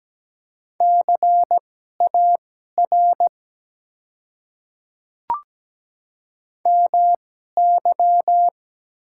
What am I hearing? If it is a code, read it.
Morse code, heard as a keyed tone, about 17 wpm, spaced as for 12 wpm, 690 Hz. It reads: CAR MY
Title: 2x Character Spacing - Top 300 Words - 17wpm